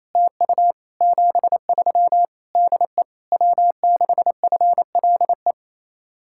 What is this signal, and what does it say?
Morse code, heard as a keyed tone, about 35 wpm, 700 Hz. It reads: TU 73 DE W6FLE
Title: Speed-Racing Common QSO Phrases Encore - 28wpm